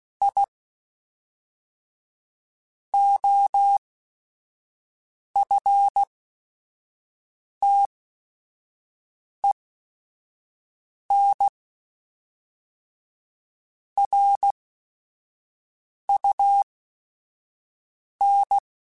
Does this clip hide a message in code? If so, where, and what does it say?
Yes: Morse code from 0.2 to 18.6 s, heard as a keyed tone, about 16 wpm, spaced as for 5 wpm, 800 Hz. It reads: I OFTEN RUN